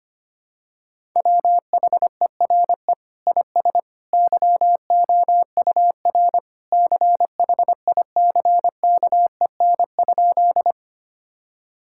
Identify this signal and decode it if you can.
Morse code, heard as a keyed tone, about 25 wpm, 700 Hz. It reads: WHERE IS YOUR CHICKEN?
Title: Sentences from Top 1000 Words - Review Part 5 - 25wpm